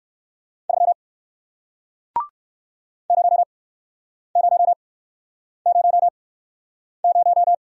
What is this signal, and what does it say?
Morse code, heard as a keyed tone, about 60 wpm, spaced as for 9 wpm, 700 Hz. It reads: 3 0000